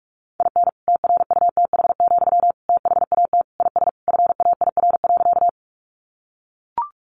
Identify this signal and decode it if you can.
Morse code, heard as a keyed tone, about 44 wpm, 700 Hz. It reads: IN TRUTH, THAT IS FAIR.
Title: Sentences from Top 1300 Words 45wpm